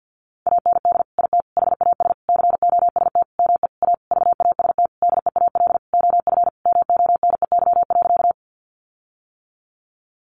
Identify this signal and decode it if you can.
Morse code, heard as a keyed tone, about 50 wpm, 700 Hz. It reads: AND IT HAS COST ME A VAST DEAL OF MONEY.
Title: Sentences from Top 1300 Words 50wpm